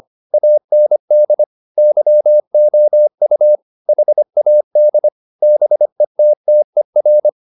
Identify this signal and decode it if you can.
Morse code, heard as a keyed tone, about 25 wpm, 600 Hz. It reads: AND YOU HAD BETTER
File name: the old man and the sea part 1 25wpm